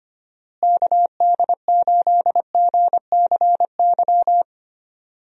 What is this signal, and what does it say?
Morse code, heard as a keyed tone, about 25 wpm, 700 Hz. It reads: KD8GCY
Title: USA Call Signs - 6 Characters - Repeated 3x - 25wpm